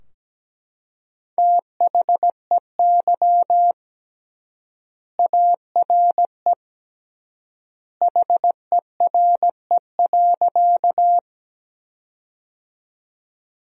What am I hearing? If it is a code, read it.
Morse code, heard as a keyed tone, about 17 wpm, 700 Hz. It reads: THEY ARE HERE.
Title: 3x Word-Spacing - Sentences from Top 200 Words - 17wpm